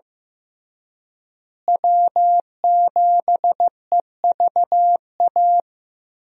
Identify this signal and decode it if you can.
Morse code, heard as a keyed tone, about 15 wpm, 700 Hz. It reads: W7EVA
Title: USA Call Signs - 5 Characters - Repeated 3x - 15wpm